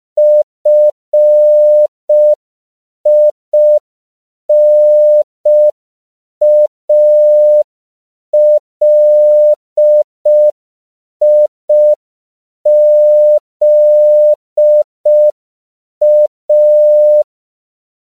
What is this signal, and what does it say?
Morse code, heard as a keyed tone, about 5 wpm, 600 Hz. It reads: FINALIZA
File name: CW / 5PPM